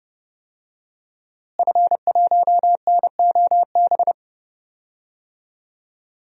Morse code, heard as a keyed tone, about 30 wpm, 700 Hz. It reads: F1NOB
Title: DXCC-100 Call Signs - 5 Characters - Repeated 3x - 30wpm